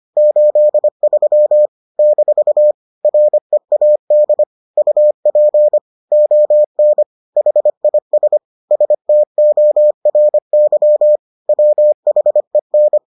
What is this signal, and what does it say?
Morse code, heard as a keyed tone, about 25 wpm, 600 Hz. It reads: 83 - READ UP ON HIS STORY WHEN